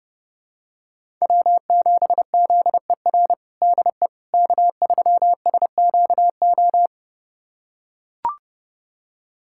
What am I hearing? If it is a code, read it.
Morse code, heard as a keyed tone, about 32 wpm, 700 Hz. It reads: W7ZER DE K3SQO
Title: USA Callsign DE Callsign - 5 Characters - 30wpm